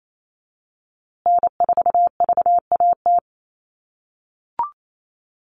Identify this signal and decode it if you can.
Morse code, heard as a keyed tone, about 28 wpm, 700 Hz. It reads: N4VAT
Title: USA Call Signs Repeated 3x 28wpm